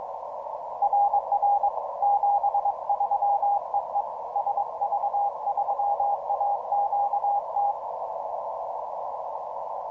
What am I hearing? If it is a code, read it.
Morse code, heard as a keyed tone, about 22 wpm, 810 Hz. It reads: RR 73 EE SV3AQT